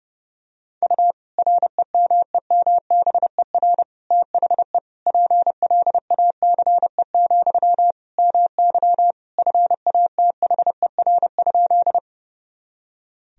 Morse code, heard as a keyed tone, about 30 wpm, 700 Hz. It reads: U REMEMBER THE PLACE, MY FATHER?
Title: Sentences from Top 400 Words and Abbreviations 30wpm